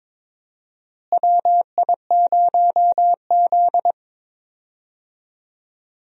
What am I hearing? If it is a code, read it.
Morse code, heard as a keyed tone, about 22 wpm, 700 Hz. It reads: WI0Z
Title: USA Call Signs - 4 Characters - Repeated 3x - 22wpm